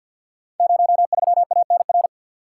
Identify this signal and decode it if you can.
Morse code, heard as a keyed tone, about 52 wpm, 690 Hz. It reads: 03ANR
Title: Code Groups - 5 Letter:Numbers - 50wpm